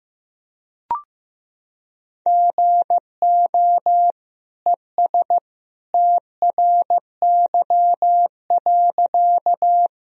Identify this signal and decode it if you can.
Morse code, heard as a keyed tone, about 15 wpm, 700 Hz. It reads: GO ES TRY.
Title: Sentences from Top 200 Words and Abbreviations 15wpm